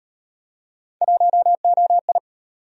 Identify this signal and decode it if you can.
Morse code, heard as a keyed tone, about 37 wpm, 700 Hz. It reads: 1OI